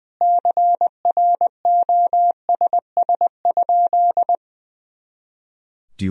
Morse code, heard as a keyed tone, about 20 wpm, 700 Hz. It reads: CROSS?
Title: Sentences from Top 300 Words and Abbreviations 20wpm